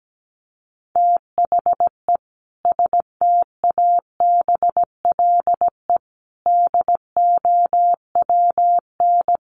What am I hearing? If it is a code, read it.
Morse code, heard as a keyed tone, about 17 wpm, 700 Hz. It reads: THE STABLE DOWN
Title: Sentences from Top 1500 Words 17wpm